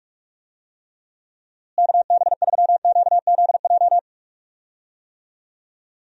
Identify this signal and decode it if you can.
Morse code, heard as a keyed tone, about 43 wpm, 700 Hz. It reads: KB3QZJ
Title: USA Call Signs - 6 Characters - Repeated 3x - 45wpm